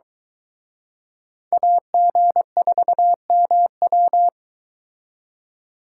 Morse code, heard as a keyed tone, about 22 wpm, 700 Hz. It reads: AG4MW